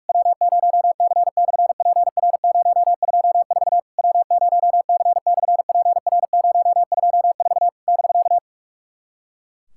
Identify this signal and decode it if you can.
Morse code, heard as a keyed tone, about 46 wpm, 700 Hz. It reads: W0C/PR024 W0C/PR024 <BK>
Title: SOTA QSO Practice 45wpm